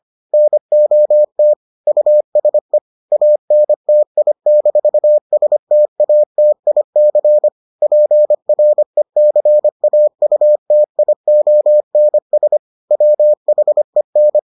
Morse code, heard as a keyed tone, about 25 wpm, 600 Hz. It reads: NOT USE ANTI-STATIC PRECAUTIONS WHEN